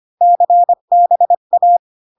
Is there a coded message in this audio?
Yes: Morse code from 0.2 to 1.8 s, heard as a keyed tone, about 25 wpm, 700 Hz. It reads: CBA